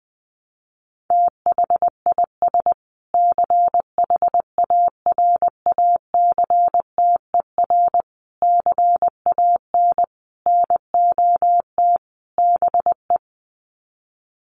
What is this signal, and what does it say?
Morse code, heard as a keyed tone, about 20 wpm, 700 Hz. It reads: THIS CHARACTER CAN NOT BE
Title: Sets of 5 Words Encore2 20wpm